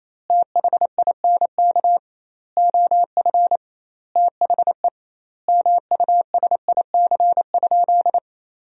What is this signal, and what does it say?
Morse code, heard as a keyed tone, about 28 wpm, 700 Hz. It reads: THINK OF THE MUSIC?
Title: 2x Word-Spacing - Sentences from Top 300 Words - 28wpm